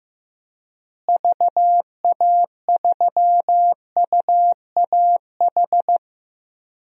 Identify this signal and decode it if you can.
Morse code, heard as a keyed tone, about 16 wpm, 700 Hz. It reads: VA3UAH
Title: DXCC-100 Call Signs - 6 Characters - Repeated 3x - 15wpm